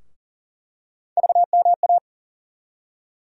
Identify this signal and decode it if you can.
Morse code, heard as a keyed tone, about 40 wpm, 700 Hz. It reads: VMA
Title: USA Call Sign Postfixes - 3 Letters 40wpm